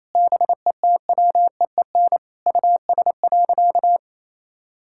Morse code, heard as a keyed tone, about 27 wpm, 700 Hz. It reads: BETWEEN US.
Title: Sentences from Top 1000 Words - Review Part 6 - 28wpm